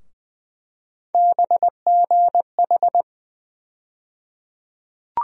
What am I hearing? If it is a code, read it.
Morse code, heard as a keyed tone, about 20 wpm, 700 Hz. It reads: BGH
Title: USA Call Sign Postfixes - 3 Letters 20wpm